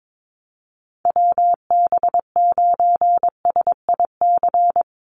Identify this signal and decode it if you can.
Morse code, heard as a keyed tone, about 22 wpm, 700 Hz. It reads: WB9SIC
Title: USA Call Signs 22wpm